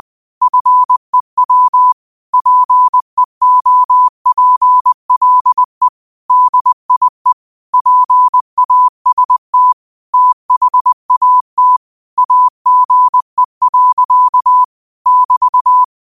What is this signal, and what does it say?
Morse code, heard as a keyed tone, about 20 wpm, 1000 Hz. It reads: FEW PEOPLE DIE PAST THAT AGE. <BT>
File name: qotdfiles / 20wpm